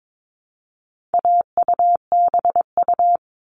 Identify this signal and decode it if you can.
Morse code, heard as a keyed tone, about 22 wpm, 700 Hz. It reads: AUBU
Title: All Cut Numbers - 4-Digit 22wpm